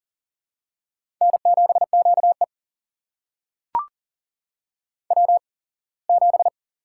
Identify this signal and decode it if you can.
Morse code, heard as a keyed tone, about 40 wpm, spaced as for 24 wpm, 700 Hz. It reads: N7QE W7
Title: USA Call Signs - 4 Characters - 8x Character Spacing - 40wpm